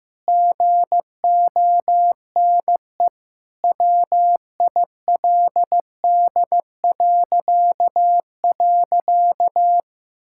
Morse code, heard as a keyed tone, about 15 wpm, 700 Hz. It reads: GONE WILD..
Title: Pangram Challenge v2 - 15wpm